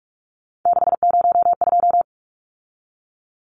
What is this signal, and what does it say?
Morse code, heard as a keyed tone, about 44 wpm, 700 Hz. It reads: 602
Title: Numbers - 3-Digit 45wpm